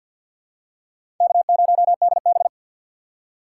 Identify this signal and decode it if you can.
Morse code, heard as a keyed tone, about 53 wpm, 700 Hz. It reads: K0DB
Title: USA Call Signs - 4 Characters - 2x Character Spacing - 50wpm